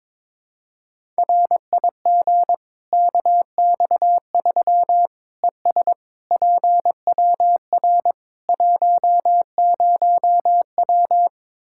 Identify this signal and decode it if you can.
Morse code, heard as a keyed tone, about 22 wpm, 700 Hz. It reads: RIG KX3 ES PWR 10W
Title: Practice QSO - Exchange 2 - Info - 22wpm